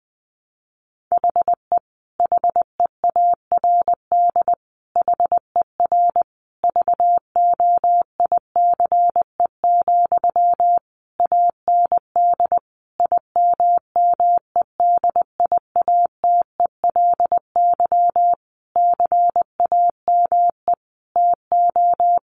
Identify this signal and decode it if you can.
Morse code, heard as a keyed tone, about 20 wpm, 700 Hz. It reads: HE HEARD HER VOICE, AND IMMEDIATELY CAME TO
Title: Sentences from Top 1100 Words 20wpm